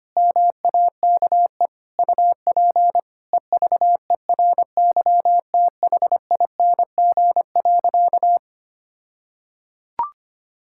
Morse code, heard as a keyed tone, about 25 wpm, 700 Hz. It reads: MAKE UP EVERYTHING.
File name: practice / 25wpm / All-Age Appropriate Jokes - 25wpm